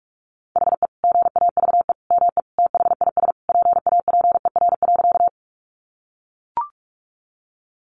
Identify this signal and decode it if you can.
Morse code, heard as a keyed tone, about 44 wpm, 700 Hz. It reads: HE GAVE ME THIS PAPER.
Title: Sentences from Top 500 Words 45wpm